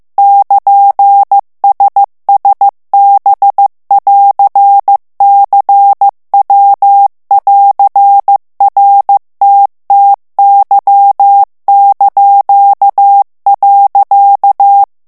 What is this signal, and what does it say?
Morse code, heard as a keyed tone, about 15 wpm, 800 Hz. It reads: <KN>SSB<AR>CW<AR>RTTY).
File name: morse mp3 / ITALIEN texte clair 2 vitesse 14 mots minute